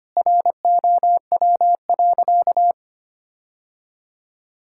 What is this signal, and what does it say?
Morse code, heard as a keyed tone, about 25 wpm, 700 Hz. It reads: ROW.
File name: practice / 25wpm / Sentences with 3-Word Phrases - 25wpm